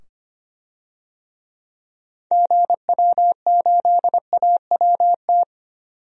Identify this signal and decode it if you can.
Morse code, heard as a keyed tone, about 25 wpm, 700 Hz. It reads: GW8AWT
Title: DXCC-100 Call Signs - 6 Characters - 25wpm